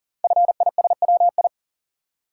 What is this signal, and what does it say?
Morse code, heard as a keyed tone, about 40 wpm, 690 Hz. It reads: FISWI
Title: Code Groups - 5 Letters - 40wpm